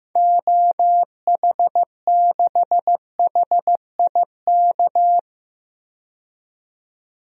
Morse code, heard as a keyed tone, about 15 wpm, 700 Hz. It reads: OH6HIK
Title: DXCC-100 Call Signs - 6 Characters - 15wpm